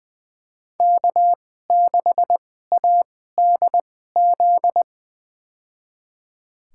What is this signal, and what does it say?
Morse code, heard as a keyed tone, about 21 wpm, spaced as for 15 wpm, 700 Hz. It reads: K6ADZ
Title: USA Call Signs - 2x Character Spacing - 20wpm